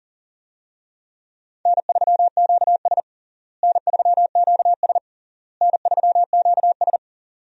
Morse code, heard as a keyed tone, about 40 wpm, 700 Hz. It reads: N3QS N3QS N3QS
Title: USA Call Signs - 4 Characters - Repeated 3x - 40wpm